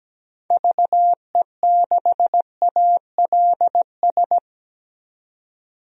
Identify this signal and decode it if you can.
Morse code, heard as a keyed tone, about 17 wpm, 700 Hz. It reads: VE6ALS
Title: DXCC-100 Call Signs - 6 Characters - 17wpm